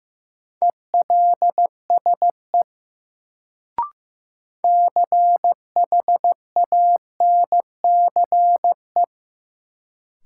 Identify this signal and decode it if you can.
Morse code, heard as a keyed tone, about 16 wpm, 700 Hz. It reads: ELSE CHANCE
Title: Sentences from Top 800 Words - New Word Drill - 15wpm